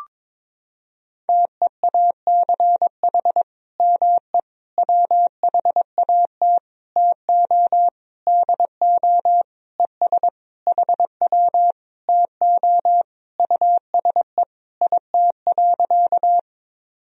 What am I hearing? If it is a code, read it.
Morse code, heard as a keyed tone, about 22 wpm, 700 Hz. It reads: TEACH ME WHAT TO DO ES HW TO USE IT.
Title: Sentences from Top 500 Words and Abbreviations 22wpm